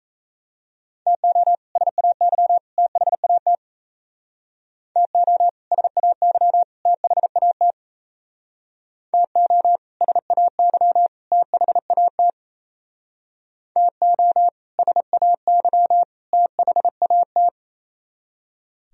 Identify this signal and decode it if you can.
Morse code, heard as a keyed tone, about 34 wpm, 700 Hz. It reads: TO SAY THAT TO SAY THAT TO SAY THAT TO SAY THAT
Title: Speed-Racing - Sets of 3 Words 28wpm